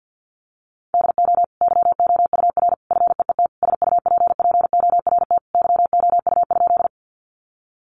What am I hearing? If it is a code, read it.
Morse code, heard as a keyed tone, about 50 wpm, 700 Hz. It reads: DO YOUR FEET SUPPORT YOU?
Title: Sentences from Top 800 Words 50wpm